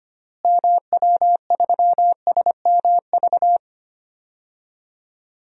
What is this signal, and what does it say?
Morse code, heard as a keyed tone, about 25 wpm, 700 Hz. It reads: MW3SMV
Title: DXCC-100 Call Signs - 6 Characters - Repeated 2x - 25wpm